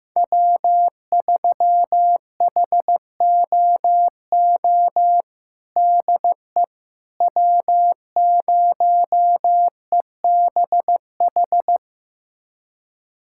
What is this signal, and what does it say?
Morse code, heard as a keyed tone, about 15 wpm, 700 Hz. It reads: W3HOO DE W0EBH
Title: USA Callsign DE Callsign - 5 Characters - 15wpm